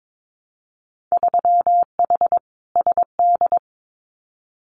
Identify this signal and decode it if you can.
Morse code, heard as a keyed tone, about 22 wpm, 700 Hz. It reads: 3H SD
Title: Winter Field Day Exchanges Encore 22wpm